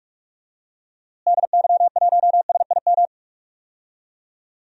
Morse code, heard as a keyed tone, about 46 wpm, 700 Hz. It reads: DY1SIM
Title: DXCC-100 Call Signs - 6 Characters - Repeated 3x - 45wpm